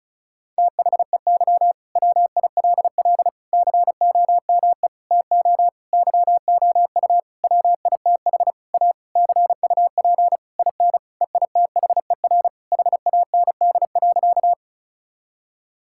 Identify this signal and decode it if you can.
Morse code, heard as a keyed tone, about 36 wpm, 700 Hz. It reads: THEY WILL COME TO YOU WITH A CUP IN EITHER HAND.
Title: Sentences from Top 1600 Words 35wpm